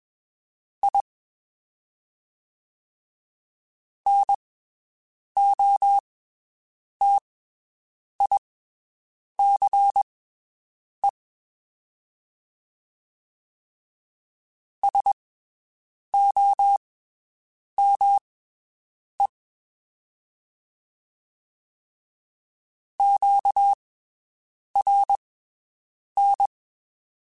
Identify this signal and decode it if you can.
Morse code, heard as a keyed tone, about 21 wpm, spaced as for 7 wpm, 800 Hz. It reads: I NOTICE SOME QRN
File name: (spaced) / QSOs Spaced-mp3s / WD0BGP 1 @20 sp